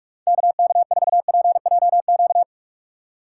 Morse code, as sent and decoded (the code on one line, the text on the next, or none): -.- -.- ....- .--. .--- --.-
KK4PJQ